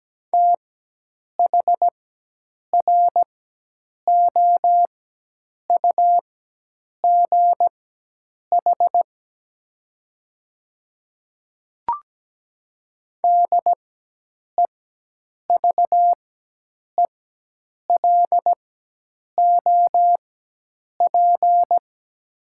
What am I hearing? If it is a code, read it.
Morse code, heard as a keyed tone, about 17 wpm, spaced as for 8 wpm, 700 Hz. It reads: THROUGH DEVELOP
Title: 4x Character Spacing - Top 500 Words - 17wpm